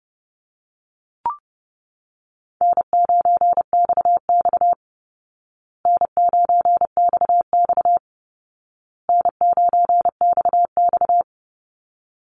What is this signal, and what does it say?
Morse code, heard as a keyed tone, about 30 wpm, 700 Hz. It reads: N9XX N9XX N9XX
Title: Easy Call Signs Repeated 3x - 30wpm